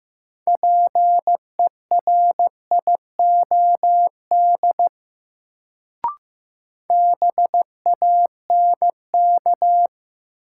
Morse code, heard as a keyed tone, about 15 wpm, 700 Hz. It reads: PERIOD BANK
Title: Sets of 5 Words - New Word Drill - 15wpm